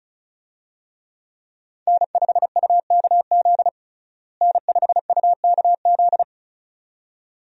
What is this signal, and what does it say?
Morse code, heard as a keyed tone, about 34 wpm, 700 Hz. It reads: N5UKZ N5UKZ
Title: USA Call Signs - 5 Characters - Repeated 2x - 35wpm